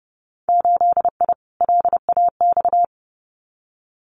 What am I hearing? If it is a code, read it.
Morse code, heard as a keyed tone, about 30 wpm, 700 Hz. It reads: 8I LAX
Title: Winter Field Day Exchanges Encore 30wpm